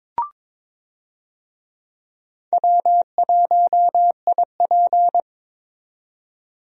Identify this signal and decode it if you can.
Morse code, heard as a keyed tone, about 22 wpm, 700 Hz. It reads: W1IP
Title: USA Call Signs - 4 Characters - 22wpm